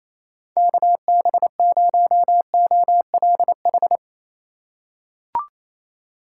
Morse code, heard as a keyed tone, about 28 wpm, 700 Hz. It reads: KB0OLH
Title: USA Call Signs - 6x Character Spacing - 28wpm